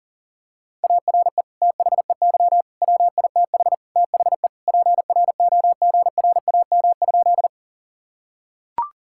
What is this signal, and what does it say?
Morse code, heard as a keyed tone, about 37 wpm, 700 Hz. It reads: ARE THEY WITH THE PROGRAM?
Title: Sentences from Top 1000 Words - Review Part 5 - 40wpm